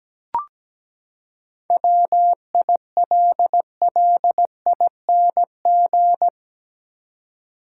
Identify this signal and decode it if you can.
Morse code, heard as a keyed tone, about 17 wpm, 700 Hz. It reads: WILLING
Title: Seven Letter Words - 17wpm